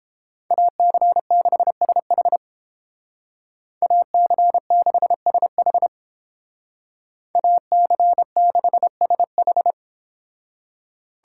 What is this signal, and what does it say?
Morse code, heard as a keyed tone, about 31 wpm, 700 Hz. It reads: AC6SH AC6SH AC6SH